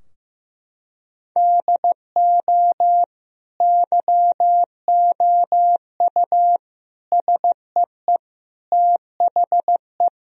Morse code, heard as a keyed tone, about 15 wpm, 700 Hz. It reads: DO YOU SEE THE
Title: Sentences from Top 1000 Words - Review Part 4 - 15wpm